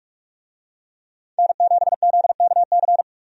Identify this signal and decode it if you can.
Morse code, heard as a keyed tone, about 46 wpm, 700 Hz. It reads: N7ZKC
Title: USA Call Signs - 5 Characters - 45wpm